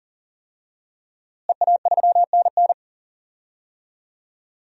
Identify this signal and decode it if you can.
Morse code, heard as a keyed tone, about 40 wpm, 700 Hz. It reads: EA3NN
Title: DXCC-100 Call Signs - 5 Characters - Repeated 2x - 40wpm